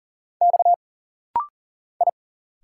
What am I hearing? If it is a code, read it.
Morse code, heard as a keyed tone, about 40 wpm, 700 Hz. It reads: X I
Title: Single Letters - Rapid-Fire 40wpm